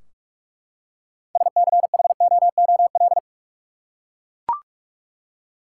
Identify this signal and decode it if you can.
Morse code, heard as a keyed tone, about 44 wpm, 700 Hz. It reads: SCHOOL
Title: Six Letter Words - 45wpm